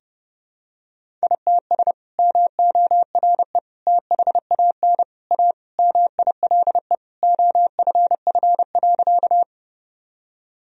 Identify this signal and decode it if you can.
Morse code, heard as a keyed tone, about 30 wpm, 700 Hz. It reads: ITS MORE THAN A MILE OFF.
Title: Sentences from Top 1000 Words - Review Part 1 - 30wpm